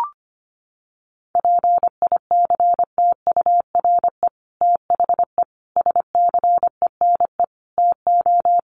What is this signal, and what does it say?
Morse code, heard as a keyed tone, about 25 wpm, 700 Hz. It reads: PICTURE THE SCENE TO
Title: Sentences from Top 1100 Words 25wpm